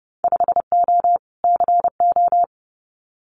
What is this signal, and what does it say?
Morse code, heard as a keyed tone, about 30 wpm, 700 Hz. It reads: 5O CO
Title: Winter Field Day Exchanges Encore 30wpm